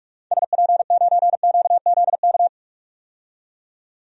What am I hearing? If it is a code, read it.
Morse code, heard as a keyed tone, about 42 wpm, 700 Hz. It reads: SP9QZK
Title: DXCC-100 Call Signs - 6 Characters - 45wpm